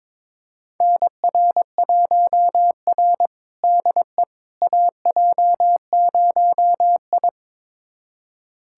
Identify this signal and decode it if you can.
Morse code, heard as a keyed tone, about 22 wpm, 700 Hz. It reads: NR1R DE AJ0I
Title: USA Callsign DE Callsign - 4 Characters - 22wpm